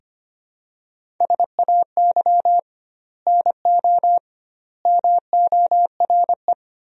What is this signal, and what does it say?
Morse code, heard as a keyed tone, about 25 wpm, 700 Hz. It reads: SAY NO MORE
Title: 2x Word-Spacing - Sentences from Top 200 Words - 25wpm